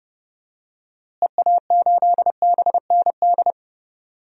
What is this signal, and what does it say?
Morse code, heard as a keyed tone, about 30 wpm, 700 Hz. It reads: EA8BND
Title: DXCC-100 Call Signs - 6 Characters - Repeated 3x - 30wpm